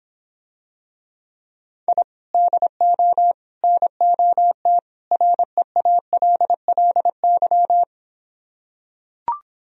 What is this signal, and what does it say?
Morse code, heard as a keyed tone, about 26 wpm, 700 Hz. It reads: I DO NOT REALLY